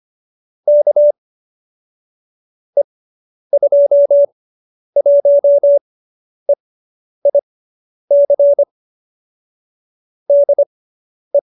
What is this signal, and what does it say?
Morse code, heard as a keyed tone, about 25 wpm, spaced as for 10 wpm, 570 Hz. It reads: K E21EIC DE